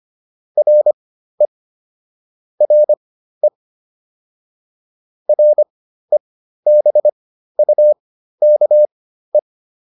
Morse code, heard as a keyed tone, about 25 wpm, spaced as for 13 wpm, 600 Hz. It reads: RE RE REBUKE